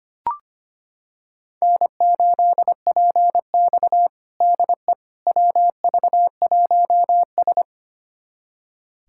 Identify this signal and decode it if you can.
Morse code, heard as a keyed tone, about 25 wpm, 700 Hz. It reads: N8PX DE WV1S
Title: USA Callsign DE Callsign - 4 Characters - 25wpm